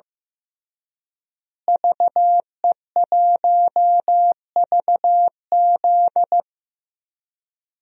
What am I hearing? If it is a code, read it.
Morse code, heard as a keyed tone, about 15 wpm, 700 Hz. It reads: VE1VZ
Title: DXCC-100 Call Signs - 5 Characters - Repeated 3x - 15wpm